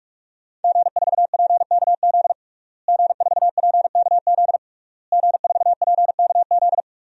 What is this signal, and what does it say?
Morse code, heard as a keyed tone, about 41 wpm, 700 Hz. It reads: G4PKZ G4PKZ G4PKZ
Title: DXCC-100 Call Signs - 5 Characters - Repeated 3x - 45wpm